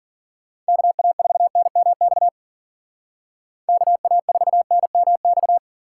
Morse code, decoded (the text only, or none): KA4NMX KA4NMX